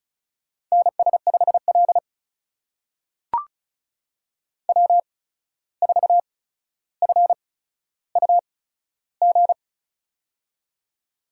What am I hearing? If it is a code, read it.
Morse code, heard as a keyed tone, about 36 wpm, spaced as for 14 wpm, 700 Hz. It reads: NS5L W4FUG
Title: USA Call Signs - 8x Character Spacing - 35wpm